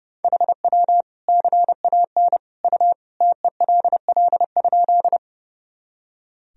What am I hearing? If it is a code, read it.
Morse code, heard as a keyed tone, about 30 wpm, 700 Hz. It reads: HW CAN U TELL?
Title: Sentences from Top 200 Words and Abbreviations 30wpm